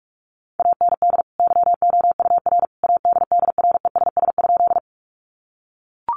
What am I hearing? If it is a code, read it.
Morse code, heard as a keyed tone, about 44 wpm, 700 Hz. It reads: AND YOUR ADDRESS?
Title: Sentences from Top 1000 Words 45wpm